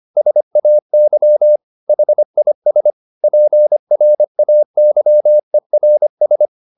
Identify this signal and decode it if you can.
Morse code, heard as a keyed tone, about 25 wpm, 600 Hz. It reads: SAY HIS PRAYERS
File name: the old man and the sea part 1 25wpm